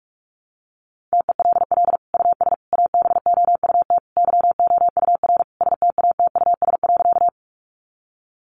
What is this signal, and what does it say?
Morse code, heard as a keyed tone, about 44 wpm, 700 Hz. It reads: TELL US ABOUT YOUR STATUS.
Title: Sentences from Top 1400 Words 45wpm